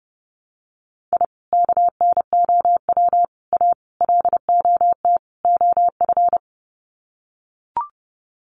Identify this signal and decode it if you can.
Morse code, heard as a keyed tone, about 30 wpm, 700 Hz. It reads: I KNOW A LOT OF
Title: Sets of 5 Words Encore 30wpm